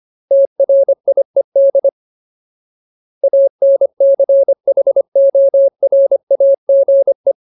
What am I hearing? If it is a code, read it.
Morse code, heard as a keyed tone, about 25 wpm, 550 Hz. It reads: TRIED ANCHORAGE
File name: Ultimate Word List / 25WPM Ultimate